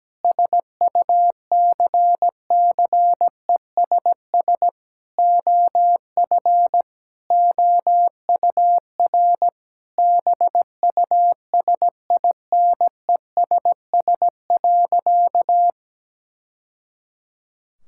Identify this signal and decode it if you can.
Morse code, heard as a keyed tone, about 17 wpm, 700 Hz. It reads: SUCCESS OF OUR BUSINESS.
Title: Sentences from Top 1000 Words - Review Part 6 - 17wpm